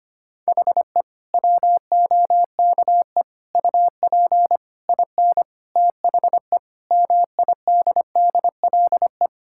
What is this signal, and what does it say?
Morse code, heard as a keyed tone, about 25 wpm, 700 Hz. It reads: HE WOKE UP IN THE MIDDLE
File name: practice / 25wpm / Sentences with 4-Word Phrases - 25wpm